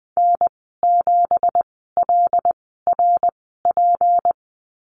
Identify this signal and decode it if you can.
Morse code, heard as a keyed tone, about 20 wpm, spaced as for 14 wpm, 700 Hz. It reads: N7LRP
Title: 167 Lesson 27g - Number 7 - Call Signs 20wpm